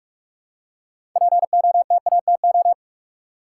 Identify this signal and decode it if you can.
Morse code, heard as a keyed tone, about 42 wpm, 700 Hz. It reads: POTATO
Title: Six Letter Words - 45wpm